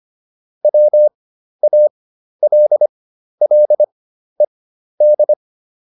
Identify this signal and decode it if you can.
Morse code, heard as a keyed tone, about 25 wpm, spaced as for 12 wpm, 600 Hz. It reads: WALLED